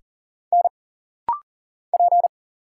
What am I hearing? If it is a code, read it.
Morse code, heard as a keyed tone, about 39 wpm, 700 Hz. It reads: N P
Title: Single Letter-Number - Rapid-Fire 40wpm